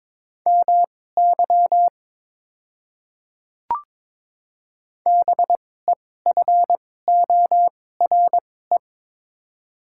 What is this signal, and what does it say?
Morse code, heard as a keyed tone, about 22 wpm, spaced as for 16 wpm, 700 Hz. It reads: MY BEFORE
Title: 2x Character Spacing - Top 200 Words - 22wpm